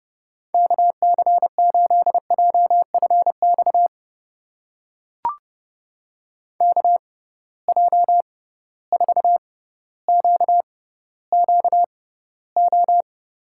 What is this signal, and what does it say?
Morse code, heard as a keyed tone, about 30 wpm, spaced as for 15 wpm, 700 Hz. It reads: KC8JFX KJ4QQO
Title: USA Call Signs - 6 Characters - 6x Character Spacing - 30wpm